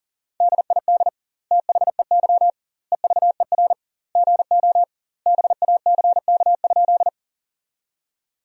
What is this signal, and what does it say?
Morse code, heard as a keyed tone, about 40 wpm, 700 Hz. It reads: DID THEY EVER GO BACK?
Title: 2x Word-Spacing - Sentences from Top 400 Words - 40wpm